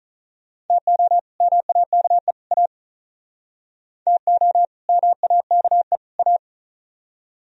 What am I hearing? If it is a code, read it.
Morse code, heard as a keyed tone, about 44 wpm, 700 Hz. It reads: TO MAKE A TO MAKE A
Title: Speed-Racing - Sets of 3 Words 35wpm